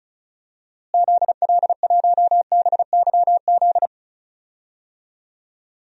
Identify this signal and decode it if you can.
Morse code, heard as a keyed tone, about 36 wpm, 700 Hz. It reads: ZL1BYZ
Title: DXCC-100 Call Signs - 6 Characters - Repeated 2x - 35wpm